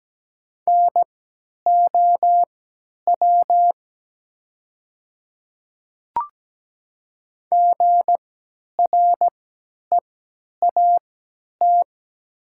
Morse code, heard as a keyed tone, about 17 wpm, spaced as for 10 wpm, 700 Hz. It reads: NOW GREAT
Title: 3x Character Spacing - Top 500 Words - 17wpm